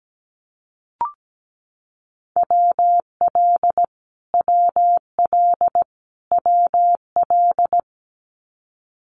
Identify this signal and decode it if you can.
Morse code, heard as a keyed tone, about 17 wpm, 700 Hz. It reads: WL WL WL
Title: Abbreviations Repeated 3x - 17wpm